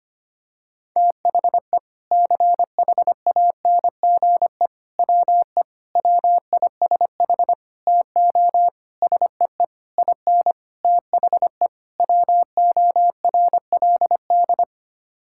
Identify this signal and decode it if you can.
Morse code, heard as a keyed tone, about 25 wpm, 700 Hz. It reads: THE CHANGE WE WISH TO SEE IN THE WORLD
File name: Sets of 9 Words - 25wpm